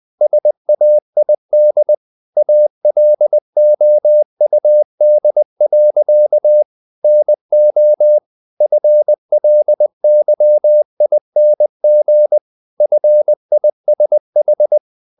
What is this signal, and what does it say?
Morse code, heard as a keyed tone, about 20 wpm, 600 Hz. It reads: SAID ALOUD. NO FLYING FISH
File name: the old man and the sea / the old man and the sea part 1 20wpm